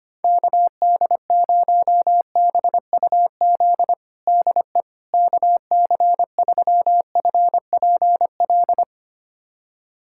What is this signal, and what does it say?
Morse code, heard as a keyed tone, about 25 wpm, 700 Hz. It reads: KD0BUZ DE KC3FPL
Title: USA Callsign DE Callsign - 6 Characters - 25wpm